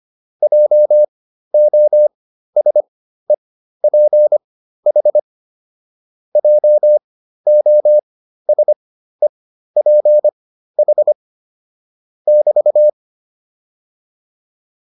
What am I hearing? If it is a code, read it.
Morse code, heard as a keyed tone, about 25 wpm, spaced as for 13 wpm, 600 Hz. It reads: JOSEPH JOSEPH <BT>
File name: QSO301 13